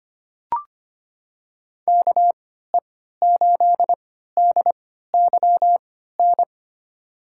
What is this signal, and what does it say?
Morse code, heard as a keyed tone, about 25 wpm, spaced as for 14 wpm, 700 Hz. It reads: KE8DYN
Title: USA Call Signs - 6 Characters - 3x Character Spacing - 25wpm